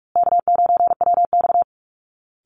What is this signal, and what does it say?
Morse code, heard as a keyed tone, about 44 wpm, 700 Hz. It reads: K9WX
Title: Call Signs Repeated 3x - 45wpm